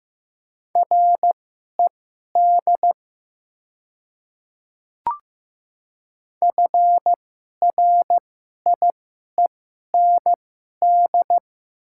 Morse code, heard as a keyed tone, about 15 wpm, spaced as for 11 wpm, 700 Hz. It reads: RED FRIEND
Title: 2x Character Spacing - Top 500 Words - 15wpm